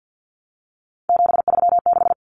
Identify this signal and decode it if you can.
Morse code, heard as a keyed tone, about 50 wpm, 700 Hz. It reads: 736
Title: Common Cut Numbers - 3-Digit 50wpm